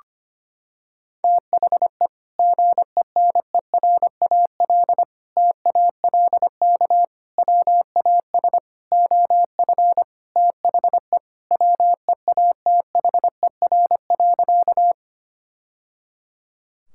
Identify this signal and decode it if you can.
Morse code, heard as a keyed tone, about 25 wpm, 700 Hz. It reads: THE GENERAL TALK WAS OF THE WEATHER.
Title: Sentences from Top 1000 Words - Review Part 5 - 25wpm